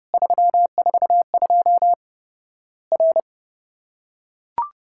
Morse code, heard as a keyed tone, about 30 wpm, 700 Hz. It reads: 342 R
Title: Worked All Europe DX Contest - QTC Exchange - 30wpm